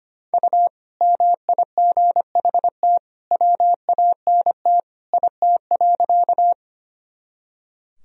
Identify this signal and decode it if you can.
Morse code, heard as a keyed tone, about 25 wpm, 700 Hz. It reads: U MIGHT WANT IT.
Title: Sentences from Top 300 Words and Abbreviations 25wpm